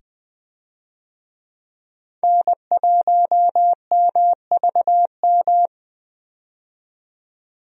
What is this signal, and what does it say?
Morse code, heard as a keyed tone, about 21 wpm, 700 Hz. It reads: N1MVM
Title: USA Call Signs - 5 Characters - 20wpm